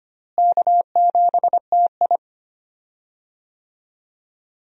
Morse code, heard as a keyed tone, about 25 wpm, 700 Hz. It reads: K7TI
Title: USA Call Signs - 4 Characters - Repeated 3x - 25wpm